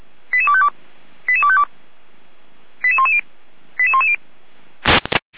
Five-tone selective call (ZVEI), heard as quick runs of five tones, tones of about 70 ms. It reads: 88252 88252 88110 88110